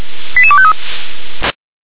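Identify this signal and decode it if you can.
Five-tone selective call (ZVEI), heard as quick runs of five tones, tones of about 70 ms. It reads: 88253
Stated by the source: ZVEI five-tone call received on an SDR.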